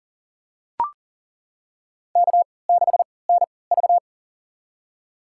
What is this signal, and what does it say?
Morse code, heard as a keyed tone, about 40 wpm, spaced as for 23 wpm, 700 Hz. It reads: K6NV
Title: USA Call Signs - 4 Characters - 3x Character Spacing - 40wpm